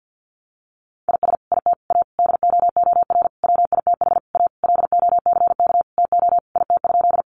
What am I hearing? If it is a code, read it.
Morse code, heard as a keyed tone, about 50 wpm, 700 Hz. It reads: IS IT A DOOR WITH A LOCK TO IT?
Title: Sentences from Top 1300 Words 50wpm